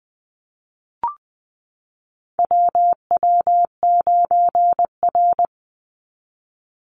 Morse code, heard as a keyed tone, about 20 wpm, 700 Hz. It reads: WW9R
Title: Call Signs Repeated 3x - 20wpm